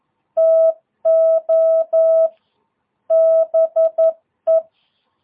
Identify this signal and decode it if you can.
Morse code, heard as a keyed tone, about 11 wpm, 650 Hz. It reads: TO BE
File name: jh8cbh / 2022cw